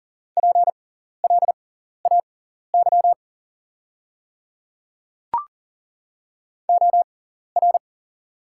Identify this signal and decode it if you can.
Morse code, heard as a keyed tone, about 40 wpm, spaced as for 14 wpm, 700 Hz. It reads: PLAY OR
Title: 6x Character Spacing - Top 200 Words - 40wpm